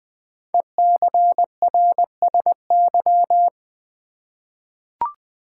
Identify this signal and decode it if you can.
Morse code, heard as a keyed tone, about 20 wpm, 700 Hz. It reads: ECRSY
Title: Code Groups - 5 Letters - 20wpm